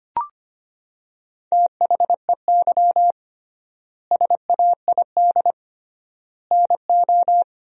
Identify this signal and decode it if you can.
Morse code, heard as a keyed tone, about 25 wpm, 700 Hz. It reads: THEY SAID NO
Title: 3x Word-Spacing - Sentences from Top 100 Words - 25wpm